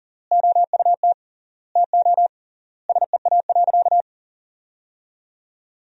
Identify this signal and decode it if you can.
Morse code, heard as a keyed tone, about 40 wpm, 700 Hz. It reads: OUT TO SEA.
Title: 3x Word-Spacing - Sentences from Top 400 Words - 40wpm